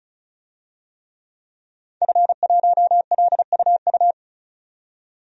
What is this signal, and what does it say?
Morse code, heard as a keyed tone, about 34 wpm, 700 Hz. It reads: F1LUU
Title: DXCC-100 Call Signs - 5 Characters - 35wpm